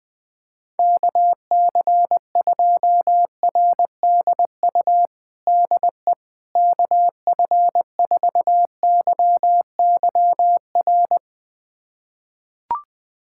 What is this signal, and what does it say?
Morse code, heard as a keyed tone, about 20 wpm, 700 Hz. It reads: KC2RDU DE KF4YYR
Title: USA Callsign DE Callsign 20wpm